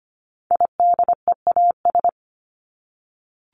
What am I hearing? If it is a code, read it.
Morse code, heard as a keyed tone, about 25 wpm, 700 Hz. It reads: IDEAS
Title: ES:S Suffix - Top 100 Words with Base Words 25wpm